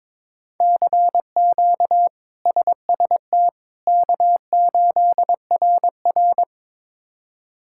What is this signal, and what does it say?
Morse code, heard as a keyed tone, about 22 wpm, 700 Hz. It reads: CQ SST K8RR
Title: K1USN SST QSO Practice - 22wpm